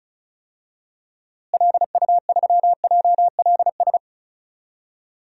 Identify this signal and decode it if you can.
Morse code, heard as a keyed tone, about 36 wpm, 700 Hz. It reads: LU3JLS
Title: DXCC-100 Call Signs - 6 Characters - Repeated 3x - 35wpm